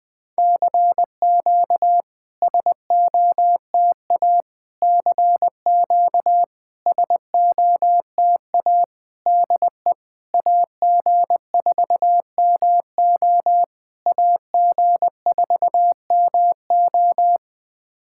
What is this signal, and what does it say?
Morse code, heard as a keyed tone, about 20 wpm, 700 Hz. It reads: CQ SOTA CQ SOTA DE AG4MO AG4MO
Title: SOTA QSO Practice 20wpm